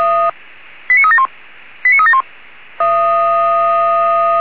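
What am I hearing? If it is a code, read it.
Five-tone selective call (ZVEI), heard as quick runs of five tones, tones of about 70 ms. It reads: 89371 89371
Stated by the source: Five-tone ZVEI selective call received on SDR.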